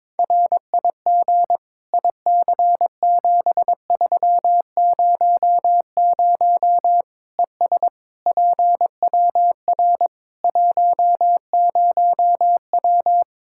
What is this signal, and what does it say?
Morse code, heard as a keyed tone, about 22 wpm, 700 Hz. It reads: RIG IC7300 ES PWR 10W
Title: K1USN SST QSO Practice 22wpm